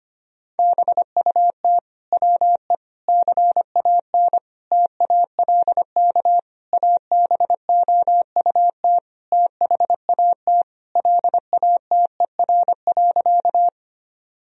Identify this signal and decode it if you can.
Morse code, heard as a keyed tone, about 25 wpm, 700 Hz. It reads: BUT WE CAN TALK ABOUT THAT LATER.
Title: Sentences from Top 1000 Words - Review Part 4 - 25wpm